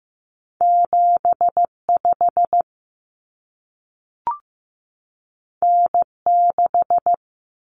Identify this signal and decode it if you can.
Morse code, heard as a keyed tone, about 15 wpm, 700 Hz. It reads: 75 N6
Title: Common Cut Numbers - 2-Digit 15wpm